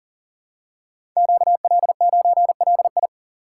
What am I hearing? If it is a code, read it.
Morse code, heard as a keyed tone, about 39 wpm, 700 Hz. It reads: QL9LI
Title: Code Groups - 5 Letter:Numbers - 40wpm